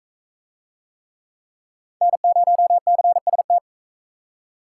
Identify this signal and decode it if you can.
Morse code, heard as a keyed tone, about 42 wpm, 700 Hz. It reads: N0CST